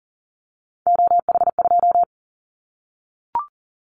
Morse code, heard as a keyed tone, about 38 wpm, 700 Hz. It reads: OH2
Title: Callsign Prefixes for Top 25 DXCC - 40wpm